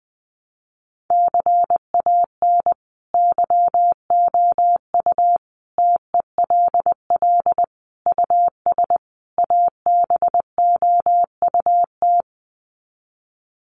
Morse code, heard as a keyed tone, about 20 wpm, 700 Hz. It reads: CAN YOU TELL US ABOUT
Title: Sets of 5 Words Encore2 20wpm